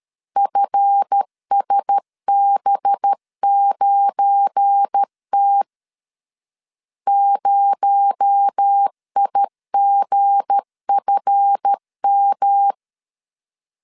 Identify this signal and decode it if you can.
Morse code, heard as a keyed tone, about 13 wpm, 800 Hz. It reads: FSB9T 0IGFM
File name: LEZIONI / 7 GF9Z / 7 GF9 12wpm 2319